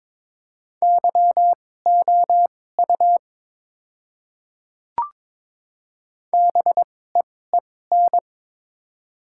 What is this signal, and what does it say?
Morse code, heard as a keyed tone, about 22 wpm, spaced as for 16 wpm, 700 Hz. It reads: YOU BEEN
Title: 2x Character Spacing - Top 100 Words - 22wpm